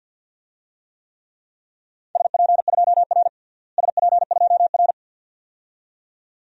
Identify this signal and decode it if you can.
Morse code, heard as a keyed tone, about 51 wpm, 700 Hz. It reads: SP2R SP2R
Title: DXCC-100 Call Signs - 4 Characters - Repeated 2x - 50wpm